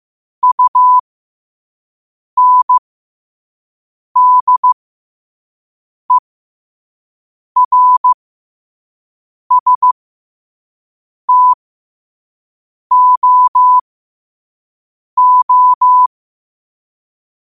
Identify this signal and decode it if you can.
Morse code, heard as a keyed tone, about 15 wpm, spaced as for 5 wpm, 1000 Hz. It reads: UNDERSTOO
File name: qotdfiles / 5wpm